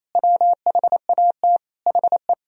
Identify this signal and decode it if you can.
Morse code, heard as a keyed tone, about 28 wpm, 700 Hz. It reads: WHAT HE
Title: Sentences from Top 400 Words and Abbreviations 28wpm